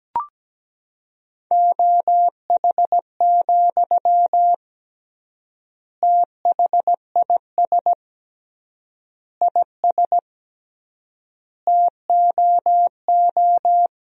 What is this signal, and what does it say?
Morse code, heard as a keyed tone, about 17 wpm, 700 Hz. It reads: OH, THIS IS TOO
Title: 3x Word-Spacing - Sentences from Top 500 Words - 17wpm